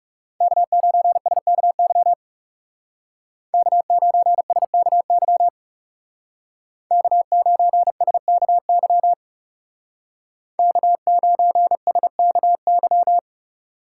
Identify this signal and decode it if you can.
Morse code, heard as a keyed tone, about 36 wpm, 700 Hz. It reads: K9SKY K9SKY K9SKY K9SKY
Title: Speed Racing - USA 1x3 Call Signs 30wpm